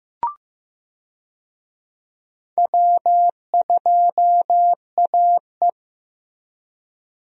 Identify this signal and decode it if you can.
Morse code, heard as a keyed tone, about 15 wpm, 700 Hz. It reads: W2AE
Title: USA Call Signs - 4 Characters - Repeated 2x - 15wpm